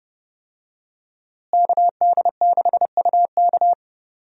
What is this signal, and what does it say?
Morse code, heard as a keyed tone, about 30 wpm, 700 Hz. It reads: KD6UK
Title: USA Call Signs - 5 Characters - Repeated 2x - 30wpm